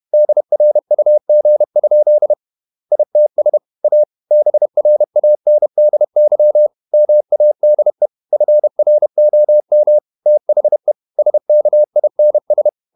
Morse code, heard as a keyed tone, about 31 wpm, 600 Hz. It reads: DRUG? ITS A BRANDY MADE FROM THE SKINS
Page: Never 31wpm | CW med Gnister